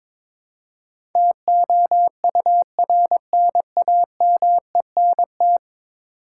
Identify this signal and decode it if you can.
Morse code, heard as a keyed tone, about 21 wpm, 700 Hz. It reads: TOURNAMENT
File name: practice / 22wpm / 10 Letter Words - First 3 Letters Given - 22wpm